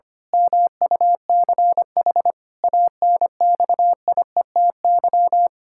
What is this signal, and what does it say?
Morse code, heard as a keyed tone, about 25 wpm, 700 Hz. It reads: MUCH ANXIETY
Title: Pangram Challenge v2 - 25wpm